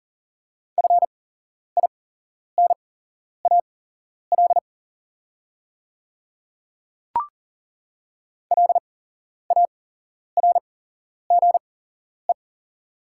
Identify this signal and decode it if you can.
Morse code, heard as a keyed tone, about 40 wpm, spaced as for 11 wpm, 700 Hz. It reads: FINAL LARGE
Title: 8x Character Spacing - Top 500 Words - 40wpm